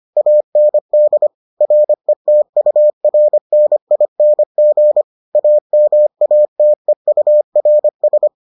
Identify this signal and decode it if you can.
Morse code, heard as a keyed tone, about 25 wpm, 600 Hz. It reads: AND RETURNING AMATEURS